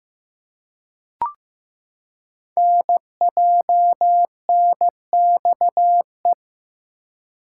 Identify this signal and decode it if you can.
Morse code, heard as a keyed tone, about 15 wpm, 700 Hz. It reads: NJNXE
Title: Code Groups - 5 Letters - 15wpm